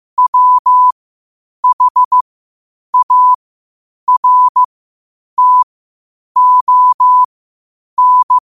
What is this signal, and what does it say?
Morse code, heard as a keyed tone, about 15 wpm, spaced as for 8 wpm, 1000 Hz. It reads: WHARTON
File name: qotdfiles / 8wpm